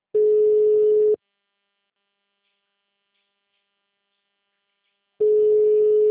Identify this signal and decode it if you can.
Ringback tone (Europe, 425 Hz).